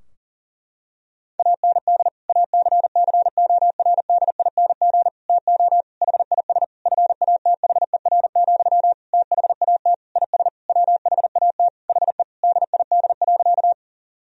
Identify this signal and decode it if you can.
Morse code, heard as a keyed tone, about 40 wpm, 700 Hz. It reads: AND ACCORDING TO HIS FATHER, THAT IS WHAT HE DID.
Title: Sentences from Top 1000 Words - Review Part 4 - 40wpm